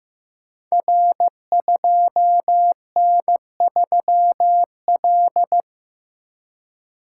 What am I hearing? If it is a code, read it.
Morse code, heard as a keyed tone, about 15 wpm, 700 Hz. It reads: R2N3L
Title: Code Groups - 5 Letter:Numbers - 15wpm